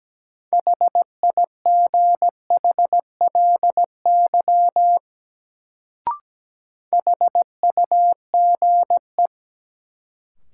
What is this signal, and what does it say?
Morse code, heard as a keyed tone, about 17 wpm, 700 Hz. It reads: HIGHLY HUGE
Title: Sentences from Top 1100 Words - New Word Drill Alt - 17wpm